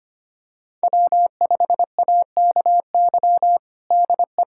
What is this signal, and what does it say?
Morse code, heard as a keyed tone, about 25 wpm, 700 Hz. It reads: W5AKY DE
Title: USA Callsign DE Callsign - 5 Characters - 25wpm